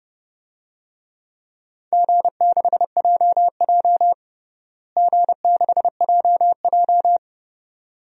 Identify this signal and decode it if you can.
Morse code, heard as a keyed tone, about 30 wpm, 700 Hz. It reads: G6JJ G6JJ
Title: DXCC-100 Call Signs - 4 Characters - Repeated 2x - 30wpm